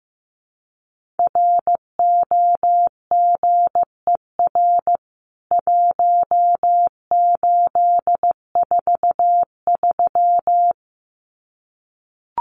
Morse code, heard as a keyed tone, about 15 wpm, 700 Hz. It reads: ROGER 1843
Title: CWT Exchange 15wpm